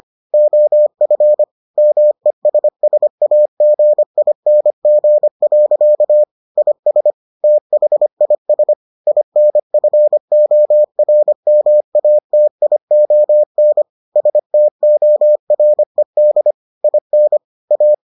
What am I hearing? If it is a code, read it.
Morse code, heard as a keyed tone, about 25 wpm, 600 Hz. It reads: OF MESSAGING. IS THIS INFORMATION STORED IN A